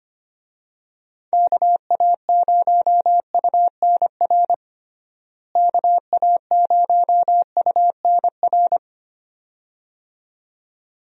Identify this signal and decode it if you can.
Morse code, heard as a keyed tone, about 25 wpm, 700 Hz. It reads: KA0UNR KA0UNR
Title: USA Call Signs - 6 Characters - Repeated 2x - 25wpm